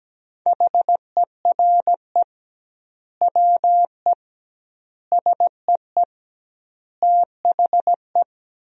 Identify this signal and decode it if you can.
Morse code, heard as a keyed tone, about 17 wpm, 700 Hz. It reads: HERE WE SEE THE
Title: 2x Word-Spacing - Sentences from Top 300 Words - 17wpm